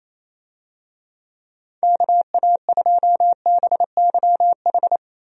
Morse code, heard as a keyed tone, about 28 wpm, 700 Hz. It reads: KA2BYH
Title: USA Call Signs - 6 Characters - Repeated 2x - 28wpm